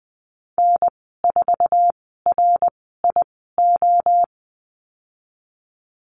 Morse code, heard as a keyed tone, about 20 wpm, spaced as for 15 wpm, 700 Hz. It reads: N4RIO
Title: 055 Lesson 10g - Letter r - Call Signs 20wpm